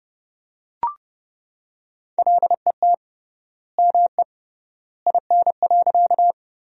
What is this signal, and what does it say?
Morse code, heard as a keyed tone, about 29 wpm, 700 Hz. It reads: LET ME IN.
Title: 3x Word-Spacing - Sentences from Top 300 Words - 30wpm